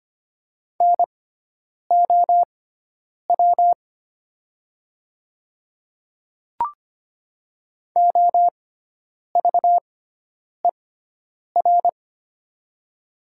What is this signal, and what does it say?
Morse code, heard as a keyed tone, about 25 wpm, spaced as for 9 wpm, 700 Hz. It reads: NOW OVER
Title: 6x Character Spacing - Top 100 Words - 25wpm